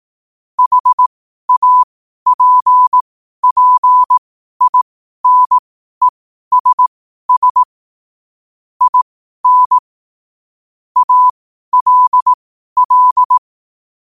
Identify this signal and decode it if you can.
Morse code, heard as a keyed tone, about 18 wpm, spaced as for 12 wpm, 1000 Hz. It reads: HAPPINESS IN ALL
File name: qotdfiles / 12wpm